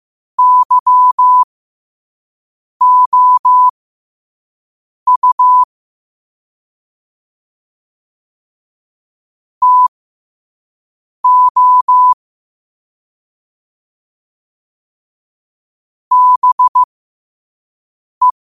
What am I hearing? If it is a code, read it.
Morse code, heard as a keyed tone, about 15 wpm, spaced as for 5 wpm, 1000 Hz. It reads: YOU TO BE
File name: qotdfiles / 5wpm